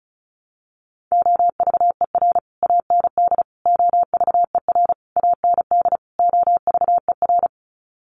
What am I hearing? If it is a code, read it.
Morse code, heard as a keyed tone, about 34 wpm, 700 Hz. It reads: OVER AND OVER AND OVER
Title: Sets of 5 Words Encore 35wpm